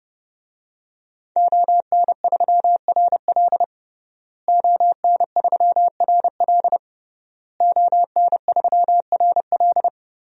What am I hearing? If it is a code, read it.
Morse code, heard as a keyed tone, about 31 wpm, 700 Hz. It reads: ON3RL ON3RL ON3RL
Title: DXCC-100 Call Signs - 5 Characters - Repeated 3x - 30wpm